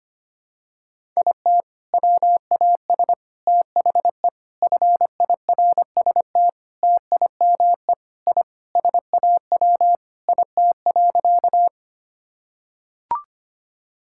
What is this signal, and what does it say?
Morse code, heard as a keyed tone, about 25 wpm, 700 Hz. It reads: IT WAS THE FIRST TIME I SAW IT.
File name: practice / 25wpm / Sentences with 3-Word Phrases - 25wpm